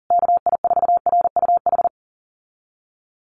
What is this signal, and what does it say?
Morse code, heard as a keyed tone, about 40 wpm, 700 Hz. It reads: KI4RUH
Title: USA Call Signs Repeated 2x 40wpm